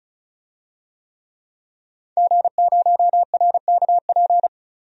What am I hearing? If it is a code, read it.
Morse code, heard as a keyed tone, about 36 wpm, 700 Hz. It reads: G0RKP
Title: DXCC-100 Call Signs - 5 Characters - 35wpm